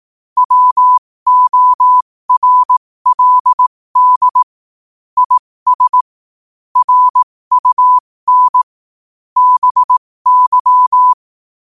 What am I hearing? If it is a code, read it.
Morse code, heard as a keyed tone, about 18 wpm, spaced as for 15 wpm, 1000 Hz. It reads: WORLD IS RUN BY
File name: qotdfiles / 15wpm